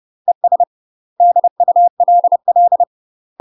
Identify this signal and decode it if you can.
Morse code, heard as a keyed tone, about 30 wpm, 700 Hz. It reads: ES DULL
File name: UK MASTER QSOs TRAINING / 30WPM / UK MASTER QSOs 0040